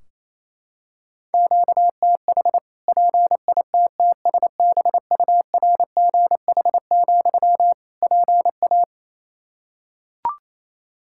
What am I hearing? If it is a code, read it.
Morse code, heard as a keyed tone, about 28 wpm, 700 Hz. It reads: QTH PITTSBURGH, PA
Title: Practice QSO - Exchange 1 - Beginning - 28wpm